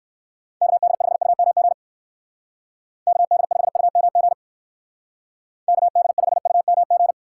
Morse code, heard as a keyed tone, about 54 wpm, 700 Hz. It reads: KD5UMG KD5UMG KD5UMG
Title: Speed Racing - USA Call Signs 45wpm